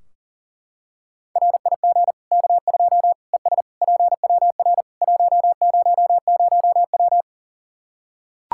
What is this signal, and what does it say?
Morse code, heard as a keyed tone, about 40 wpm, 700 Hz. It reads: RIG K2 ES PWR 100W
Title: Practice QSO - Exchange 2 - Info - 40wpm